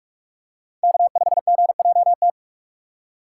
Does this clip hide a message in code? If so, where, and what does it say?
Yes: Morse code from 0.8 to 2.3 s, heard as a keyed tone, about 44 wpm, 700 Hz. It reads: K5GJT